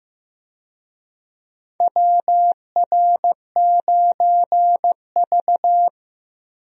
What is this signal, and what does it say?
Morse code, heard as a keyed tone, about 16 wpm, 700 Hz. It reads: WR9V
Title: USA Call Signs - 4 Characters - Repeated 3x - 15wpm